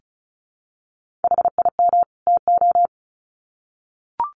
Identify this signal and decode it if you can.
Morse code, heard as a keyed tone, about 34 wpm, 700 Hz. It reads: HIM TO
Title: Sets of 2 Words 35wpm